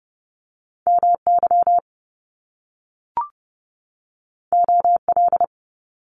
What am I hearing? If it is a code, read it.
Morse code, heard as a keyed tone, about 30 wpm, 700 Hz. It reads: MY OL
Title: Top Two Letter Combinations in Words 30wpm